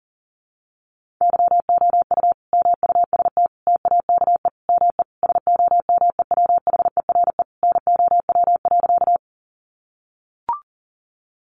Morse code, heard as a keyed tone, about 40 wpm, 700 Hz. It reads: YOU MUST TAKE ME SOMEWHERE NOW.
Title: Sentences from Top 1400 Words 40wpm